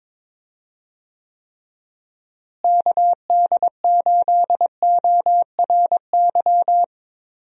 Morse code, heard as a keyed tone, about 22 wpm, 700 Hz. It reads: KD8ORY
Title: USA Call Signs - 6 Characters - Repeated 2x - 22wpm